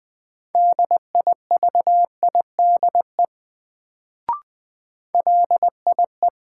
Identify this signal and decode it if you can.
Morse code, heard as a keyed tone, about 21 wpm, 700 Hz. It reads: DIVIDE LIE
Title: Sentences from Top 1600 Words - New Word Drill Alt - 20wpm